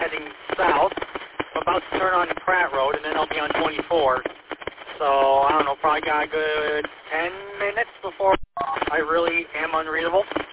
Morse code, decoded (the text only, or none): none